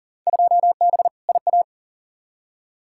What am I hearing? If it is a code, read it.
Morse code, heard as a keyed tone, about 40 wpm, 700 Hz. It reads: 2B IA
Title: Field Day Exchanges Encore - 40wpm